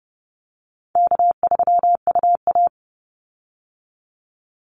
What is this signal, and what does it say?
Morse code, heard as a keyed tone, about 30 wpm, 700 Hz. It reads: K3UA
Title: Easy Call Signs - 30wpm